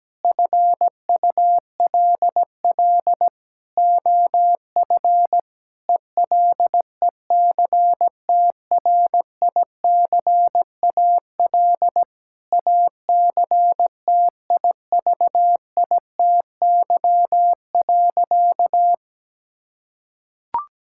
Morse code, heard as a keyed tone, about 17 wpm, 700 Hz. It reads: FULL OF ELECTRICAL ACTIVITY.
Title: Sentences from Top 1000 Words - Review Part 6 - 17wpm